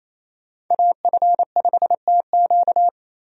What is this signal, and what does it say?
Morse code, heard as a keyed tone, about 28 wpm, 700 Hz. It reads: AF5TQ
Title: USA Call Signs - 5 Characters - Repeated 2x - 28wpm